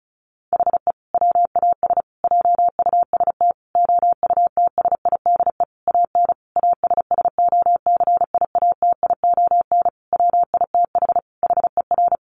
Morse code, heard as a keyed tone, about 34 wpm, 700 Hz. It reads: HE WAS JUST OUTSIDE AN ASSOCIATION WITH HER
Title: Sentences from Top 1100 Words 35wpm